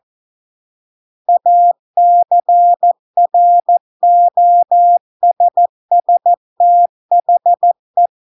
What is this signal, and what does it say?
Morse code, heard as a keyed tone, about 14 wpm, 700 Hz. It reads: ACROSS THE